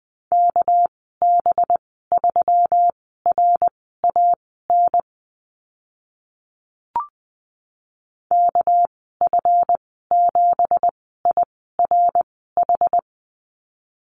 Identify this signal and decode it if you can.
Morse code, heard as a keyed tone, about 20 wpm, spaced as for 15 wpm, 700 Hz. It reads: KB3RAN KF7IRH
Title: 200 Lesson 32g - Letter k - Call Signs 20wpm